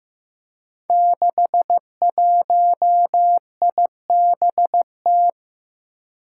Morse code, heard as a keyed tone, about 15 wpm, 700 Hz. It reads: 61IBT
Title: Five Challenging Letter-Number Combinations 15wpm